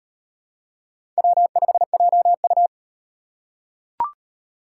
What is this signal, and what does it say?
Morse code, heard as a keyed tone, about 38 wpm, 700 Hz. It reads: W5JU